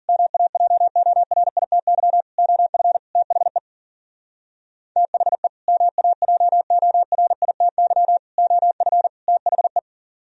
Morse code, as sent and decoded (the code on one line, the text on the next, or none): -- .- .--- --- .-. .. - -.-- / --- ..-. / - .... / - .... / -- .- .--- --- .-. .. - -.-- / --- ..-. / - .... .
MAJORITY OF THE THE MAJORITY OF THE